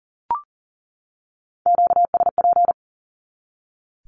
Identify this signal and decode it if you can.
Morse code, heard as a keyed tone, about 40 wpm, 700 Hz. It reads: QSP
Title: Q-Codes 40wpm